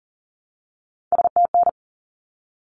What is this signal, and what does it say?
Morse code, heard as a keyed tone, about 40 wpm, 700 Hz. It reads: STN
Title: Abbreviations 40wpm